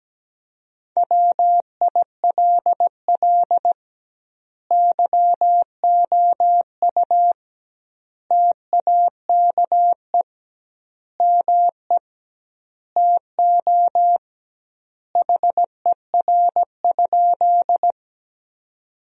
Morse code, heard as a keyed tone, about 17 wpm, 700 Hz. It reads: WILL YOU TAKE ME TO HER?
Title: 2x Word-Spacing - Sentences from Top 200 Words - 17wpm